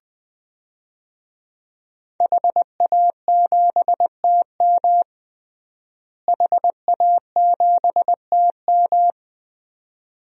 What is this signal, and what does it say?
Morse code, heard as a keyed tone, about 20 wpm, 700 Hz. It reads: HA7TM HA7TM
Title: DXCC-100 Call Signs - 5 Characters - Repeated 2x - 20wpm